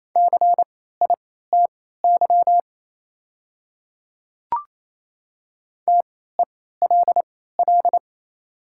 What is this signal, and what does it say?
Morse code, heard as a keyed tone, about 28 wpm, spaced as for 16 wpm, 700 Hz. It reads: CITY TELL
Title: 3x Character Spacing - Top 500 Words - 28wpm